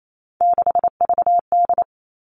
Morse code, heard as a keyed tone, about 28 wpm, 700 Hz. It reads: 6VD
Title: All Cut Numbers - 3-Digit 28wpm